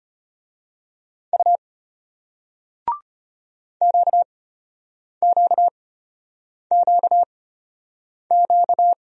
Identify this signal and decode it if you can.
Morse code, heard as a keyed tone, about 33 wpm, spaced as for 8 wpm, 700 Hz. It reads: U QQQQ